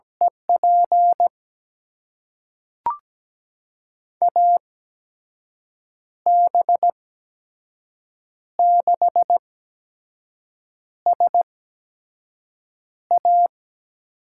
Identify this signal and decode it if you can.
Morse code, heard as a keyed tone, about 17 wpm, spaced as for 5 wpm, 700 Hz. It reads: EP AB6SA
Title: USA Call Signs - 8x Character Spacing - 17wpm